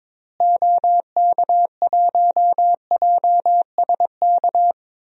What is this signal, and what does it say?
Morse code, heard as a keyed tone, about 22 wpm, 700 Hz. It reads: OK1JSK
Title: DXCC-100 Call Signs - 6 Characters - Repeated 2x - 22wpm